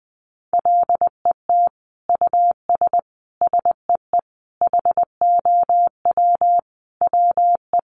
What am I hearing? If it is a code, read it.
Morse code, heard as a keyed tone, about 20 wpm, 700 Hz. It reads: LET US SEE HOW WE
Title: Sentences from Top 1400 Words 20wpm